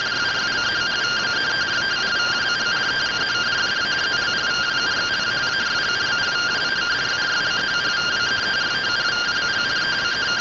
Two-tone teleprinter signal, about 170 Hz shift.
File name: findings / SAGITAL fr15 nb203 3mm.gen-23 rtty rx